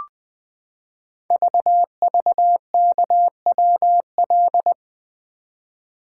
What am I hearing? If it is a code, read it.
Morse code, heard as a keyed tone, about 20 wpm, 700 Hz. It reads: VVKWL
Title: Code Groups - 5 Letters - 20wpm